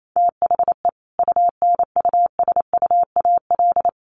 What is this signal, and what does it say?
Morse code, heard as a keyed tone, about 28 wpm, 700 Hz. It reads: THE UNUSUAL
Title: Sentences from Top 1400 Words 28wpm